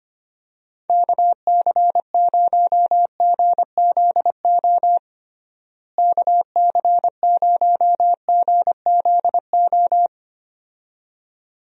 Morse code, heard as a keyed tone, about 25 wpm, 700 Hz. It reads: KC0GZO KC0GZO
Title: USA Call Signs - 6 Characters - Repeated 2x - 25wpm